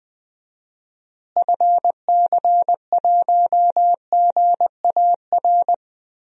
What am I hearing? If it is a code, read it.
Morse code, heard as a keyed tone, about 20 wpm, 700 Hz. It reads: FC1GAR
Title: DXCC-100 Call Signs - 6 Characters - Repeated 3x - 20wpm